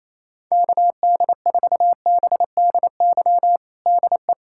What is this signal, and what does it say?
Morse code, heard as a keyed tone, about 28 wpm, 700 Hz. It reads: KD4BDY DE
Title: USA Callsign DE Callsign 28wpm